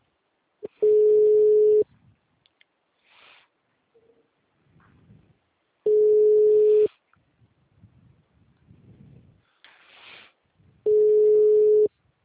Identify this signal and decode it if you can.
Ringback tone (Europe, 425 Hz).